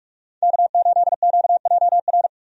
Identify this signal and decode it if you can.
Morse code, heard as a keyed tone, about 46 wpm, 700 Hz. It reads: K8QJR
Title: USA Call Signs - 5 Characters - 45wpm